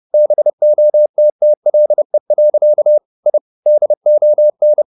Morse code, heard as a keyed tone, about 30 wpm, 600 Hz. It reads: BOTTLE. I DON
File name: the old man and the sea / the old man and the sea 1 30wpm